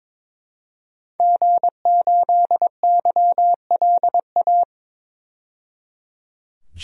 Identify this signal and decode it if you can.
Morse code, heard as a keyed tone, about 22 wpm, 700 Hz. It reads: G8YLA
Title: DXCC-100 Call Signs - 5 Characters - 22wpm